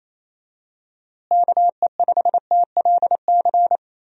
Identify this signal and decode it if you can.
Morse code, heard as a keyed tone, about 28 wpm, 700 Hz. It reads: KE5TLC
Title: USA Call Signs - 6 Characters - Repeated 3x - 28wpm